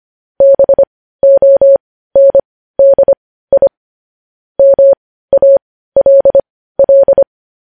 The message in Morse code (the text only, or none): BONDI MALL